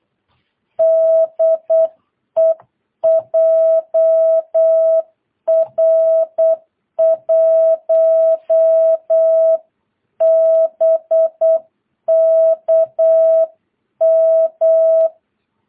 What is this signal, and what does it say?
Morse code, heard as a keyed tone, about 8 wpm, 660 Hz. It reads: DE JR1BKM
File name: jh8cbh / 2022cw